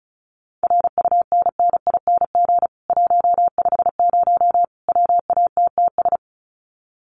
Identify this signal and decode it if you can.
Morse code, heard as a keyed tone, about 34 wpm, 700 Hz. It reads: RUNNING 150 WATTS
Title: Common QSO Phrases 35wpm